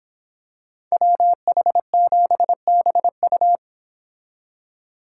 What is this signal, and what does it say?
Morse code, heard as a keyed tone, about 25 wpm, 700 Hz. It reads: WH7BU